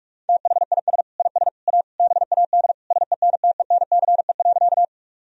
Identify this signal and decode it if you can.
Morse code, heard as a keyed tone, about 46 wpm, 700 Hz. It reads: THIS IS A BAD SENTENCE.
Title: Sentences from Top 1000 Words - Review Part 1 - 45wpm